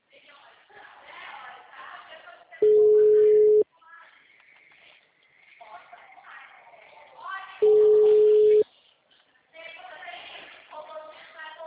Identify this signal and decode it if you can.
Ringback tone (Europe, 425 Hz), with speech.